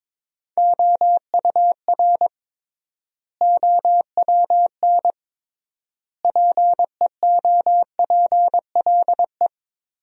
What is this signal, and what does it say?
Morse code, heard as a keyed tone, about 22 wpm, 700 Hz. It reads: OUR OWN PEOPLE
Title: 3x Word-Spacing - Sentences from Top 300 Words - 22wpm